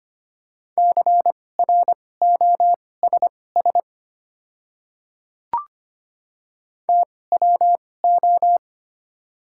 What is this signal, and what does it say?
Morse code, heard as a keyed tone, about 25 wpm, spaced as for 18 wpm, 700 Hz. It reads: CROSS TWO
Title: 2x Character Spacing - Top 300 Words - 25wpm